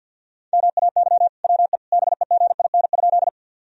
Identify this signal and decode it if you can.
Morse code, heard as a keyed tone, about 51 wpm, 700 Hz. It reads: MAY WE BEGIN?
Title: Sentences from Top 1000 Words - Review Part 1 - 50wpm